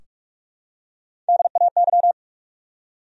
Morse code, heard as a keyed tone, about 46 wpm, 700 Hz. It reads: DAY
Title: Top 100 Words 45wpm